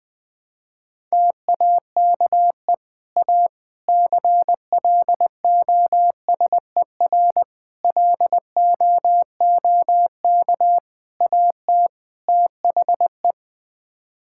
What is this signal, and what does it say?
Morse code, heard as a keyed tone, about 20 wpm, 700 Hz. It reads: TAKE A CLOSER LOOK AT THE
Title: Sets of 6 Words - 20wpm